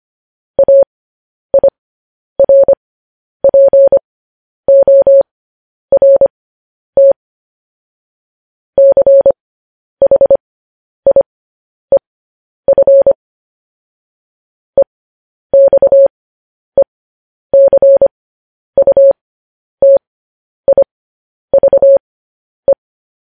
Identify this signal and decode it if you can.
Morse code, heard as a keyed tone, about 25 wpm, spaced as for 10 wpm, 550 Hz. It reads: AIRPORT CHIEF EXECUTIVE
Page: news headlines in morse code at 10 wpm